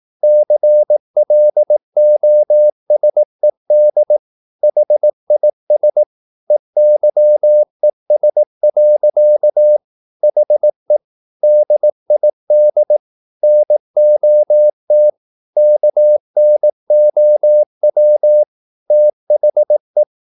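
Morse code, heard as a keyed tone, about 18 wpm, 600 Hz. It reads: CLOSED HIS EYES. HE DID NOT KNOW THE
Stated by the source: Morse practice at 18 wpm, English text, from CW med Gnister.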